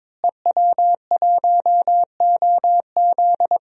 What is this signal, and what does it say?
Morse code, heard as a keyed tone, about 22 wpm, 700 Hz. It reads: EW1OZ
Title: DXCC-100 Call Signs - 5 Characters - Repeated 3x - 22wpm